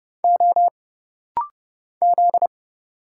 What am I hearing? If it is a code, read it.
Morse code, heard as a keyed tone, about 30 wpm, 700 Hz. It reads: O Z
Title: Single Letters - Rapid-Fire 30wpm